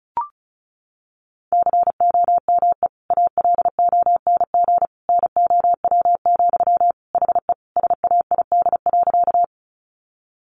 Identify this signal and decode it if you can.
Morse code, heard as a keyed tone, about 34 wpm, 700 Hz. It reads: COME ALONG NOW, HE SAID.
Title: Sentences from Top 700 Words 35wpm